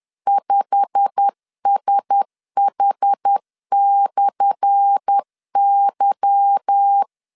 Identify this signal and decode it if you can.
Morse code, heard as a keyed tone, about 10 wpm, 800 Hz. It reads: 5SH/Y
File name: LEZIONI / 10 SIGNS / 10A SLASH 10wpm 2808